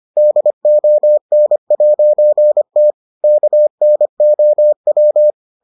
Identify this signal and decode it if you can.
Morse code, heard as a keyed tone, about 25 wpm, 600 Hz. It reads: DON'T KNOW